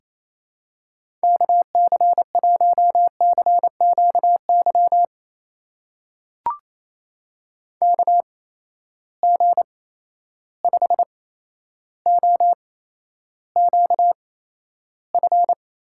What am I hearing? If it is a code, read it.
Morse code, heard as a keyed tone, about 28 wpm, spaced as for 12 wpm, 700 Hz. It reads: KC1CQY KG5OQF
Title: USA Call Signs - 6 Characters - 8x Character Spacing - 28wpm